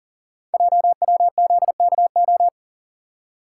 Morse code, heard as a keyed tone, about 42 wpm, 690 Hz. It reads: JWZKO
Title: Code Groups - 5 Letters - 40wpm